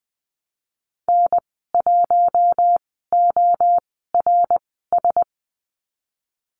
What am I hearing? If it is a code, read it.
Morse code, heard as a keyed tone, about 19 wpm, spaced as for 14 wpm, 700 Hz. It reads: N1ORS
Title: 076 Lesson 13g - Letter L - Call Signs 20wpm